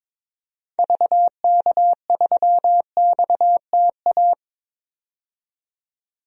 Morse code, heard as a keyed tone, about 22 wpm, 700 Hz. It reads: VK3XTA
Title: DXCC-100 Call Signs - 6 Characters - 22wpm